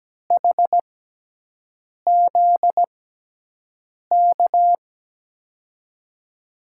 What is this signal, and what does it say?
Morse code, heard as a keyed tone, about 17 wpm, spaced as for 6 wpm, 700 Hz. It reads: HZK
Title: USA Call Signs - 6x Character Spacing - 17wpm